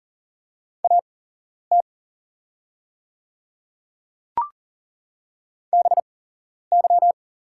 Morse code, heard as a keyed tone, about 40 wpm, spaced as for 11 wpm, 700 Hz. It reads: AT BY
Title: 8x Character Spacing - Top 300 Words - 40wpm